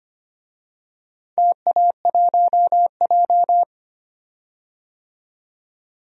Morse code, heard as a keyed tone, about 25 wpm, 690 Hz. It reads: TA1J
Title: DXCC-100 Call Signs - 4 Characters - 25wpm